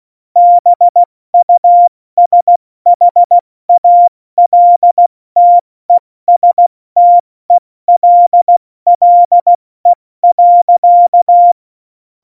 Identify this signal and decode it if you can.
Morse code, heard as a keyed tone, about 16 wpm, spaced as for 14 wpm, 700 Hz. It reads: BUSHALTESTELLE.
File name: Morsekurs / Witze / Witz 18 Juli 17, Tempo 70 F80